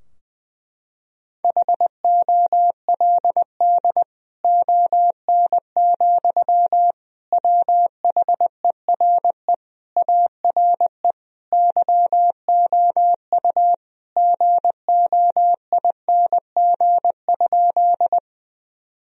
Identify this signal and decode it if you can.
Morse code, heard as a keyed tone, about 20 wpm, 700 Hz. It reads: HOLD ON, WHERE ARE YOU GOING?
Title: Sentences from Top 1000 Words - Review Part 2 - 20wpm